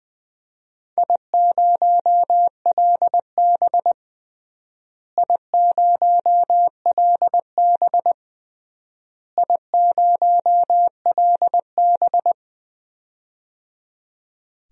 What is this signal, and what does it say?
Morse code, heard as a keyed tone, about 20 wpm, 700 Hz. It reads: I0LB I0LB I0LB
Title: DXCC-100 Call Signs - 4 Characters - Repeated 3x - 20wpm